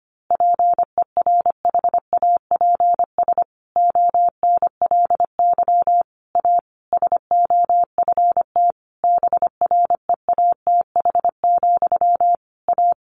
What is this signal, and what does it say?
Morse code, heard as a keyed tone, about 25 wpm, 700 Hz. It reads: PERHAPS ONLY A SOFT BREATH, A
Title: A Sound Of Thunder 25wpm